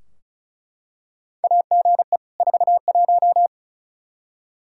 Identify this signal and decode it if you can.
Morse code, heard as a keyed tone, about 36 wpm, 700 Hz. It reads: AGE 41
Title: K1USN SST QSO Practice 35wpm